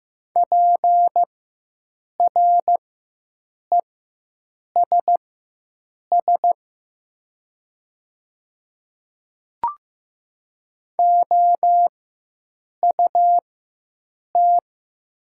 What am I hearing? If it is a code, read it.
Morse code, heard as a keyed tone, about 15 wpm, spaced as for 7 wpm, 700 Hz. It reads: PRESS OUT
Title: 4x Character Spacing - Top 300 Words - 15wpm